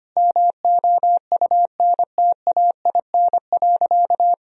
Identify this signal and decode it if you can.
Morse code, heard as a keyed tone, about 25 wpm, 700 Hz. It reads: MOUNTAIN.
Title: Sentences from Top 1000 Words - Review Part 1 - 25wpm